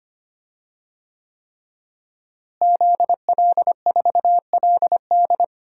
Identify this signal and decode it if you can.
Morse code, heard as a keyed tone, about 25 wpm, 700 Hz. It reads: ZL4LD
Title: DXCC-100 Call Signs - 5 Characters - Repeated 3x - 25wpm